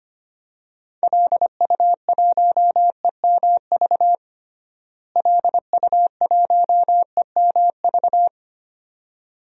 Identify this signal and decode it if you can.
Morse code, heard as a keyed tone, about 25 wpm, 700 Hz. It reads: LU1EMV LU1EMV
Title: DXCC-100 Call Signs - 6 Characters - Repeated 2x - 25wpm